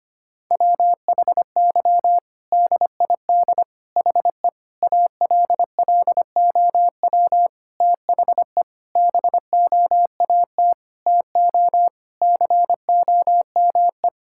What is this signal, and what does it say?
Morse code, heard as a keyed tone, about 25 wpm, 700 Hz. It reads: WHY DID HE ALLOW THE BOAT TO COME
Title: Sentences from Top 1000 Words - Review Part 5 - 25wpm